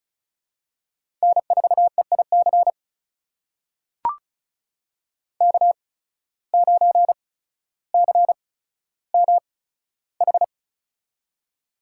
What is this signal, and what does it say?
Morse code, heard as a keyed tone, about 36 wpm, spaced as for 15 wpm, 700 Hz. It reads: N4EIC K9CMH
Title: USA Call Signs - 8x Character Spacing - 35wpm